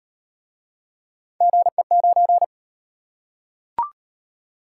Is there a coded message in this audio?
Yes: Morse code from 1.4 to 2.4 s, heard as a keyed tone, about 38 wpm, 700 Hz. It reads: GE9